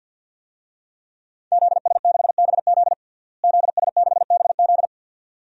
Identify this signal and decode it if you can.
Morse code, heard as a keyed tone, about 47 wpm, 700 Hz. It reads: ZS6BZ ZS6BZ
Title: DXCC-100 Call Signs - 5 Characters - Repeated 2x - 50wpm